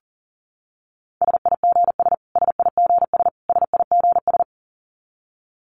Morse code, heard as a keyed tone, about 40 wpm, 700 Hz. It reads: SIGS SIGS SIGS
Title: Abbreviations Repeated 3x - 40wpm